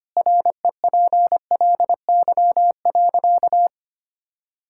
Morse code, heard as a keyed tone, about 25 wpm, 700 Hz. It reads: REPLY.
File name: Easy Sentences with Hints - 25wpm